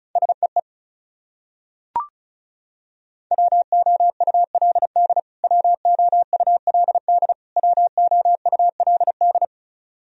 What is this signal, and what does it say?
Morse code, heard as a keyed tone, about 34 wpm, 700 Hz. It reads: SEE WOULD WOULD WOULD
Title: Top 100 Words Repeated 35wpm